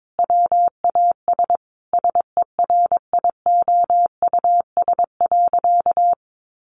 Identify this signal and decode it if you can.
Morse code, heard as a keyed tone, about 22 wpm, 700 Hz. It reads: WAS SERIOUS.
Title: Sentences from Top 1100 Words 22wpm